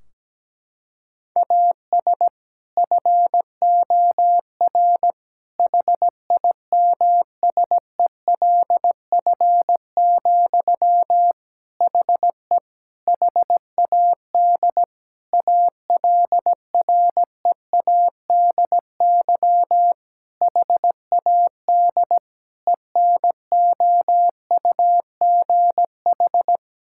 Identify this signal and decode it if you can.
Morse code, heard as a keyed tone, about 17 wpm, 700 Hz. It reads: AS FOR HIMSELF, HE HAD ALREADY HAD ENOUGH
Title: Sentences from Top 1600 Words 17wpm